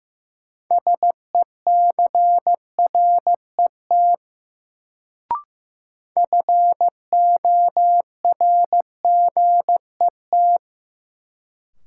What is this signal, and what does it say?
Morse code, heard as a keyed tone, about 15 wpm, 700 Hz. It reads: SECRET FORGET
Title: Sentences from Top 1200 Words - New Word Drill - 15wpm